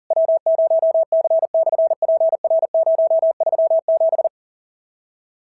Morse code, heard as a keyed tone, about 40 wpm, 690 Hz. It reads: W0C/PR037
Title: SOTA QSO Practice 40wpm